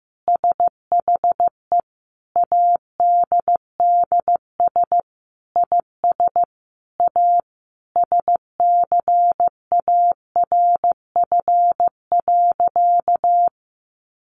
Morse code, heard as a keyed tone, about 15 wpm, 700 Hz. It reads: SHE ADDS IS A SCARF.
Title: Easy Phrases 15wpm